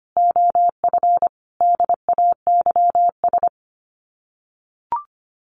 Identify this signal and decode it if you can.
Morse code, heard as a keyed tone, about 25 wpm, 700 Hz. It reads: OF DAYS
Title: Sets of 5 Words Encore 25wpm